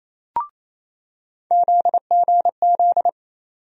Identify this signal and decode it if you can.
Morse code, heard as a keyed tone, about 28 wpm, 700 Hz. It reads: ZGZ
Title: USA Call Sign Postfixes - 3 Letters 28wpm